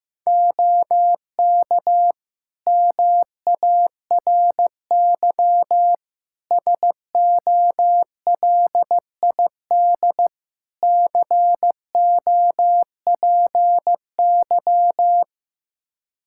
Morse code, heard as a keyed tone, about 15 wpm, 700 Hz. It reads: OK MARY SOLID COPY
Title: Practice QSO - Exchange 3 - More Info - 15wpm